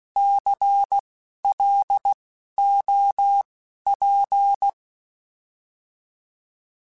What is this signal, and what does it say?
Morse code, heard as a keyed tone, about 16 wpm, spaced as for 11 wpm, 800 Hz. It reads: CLOP
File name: Alphabetical / Alphabetical mp3s / Cs @10